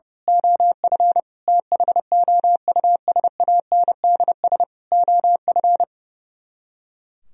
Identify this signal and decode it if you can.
Morse code, heard as a keyed tone, about 28 wpm, 700 Hz. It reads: OF THOUSANDS OF
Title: Speed-Racing - Sets of 4 Words 30wpm